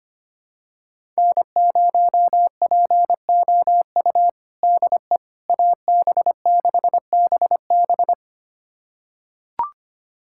Morse code, heard as a keyed tone, about 25 wpm, 700 Hz. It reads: N0POU DE AB6BB
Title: USA Callsign DE Callsign - 5 Characters - 25wpm